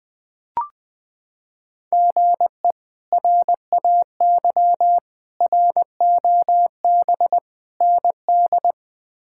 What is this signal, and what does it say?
Morse code, heard as a keyed tone, about 20 wpm, 700 Hz. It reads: GE RAY ROB ND
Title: K1USN SST QSO Practice - 20wpm